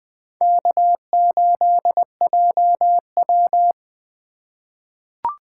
Morse code, heard as a keyed tone, about 20 wpm, 700 Hz. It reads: K8JW
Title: USA Call Signs - 4 Characters - 3x Character Spacing - 20wpm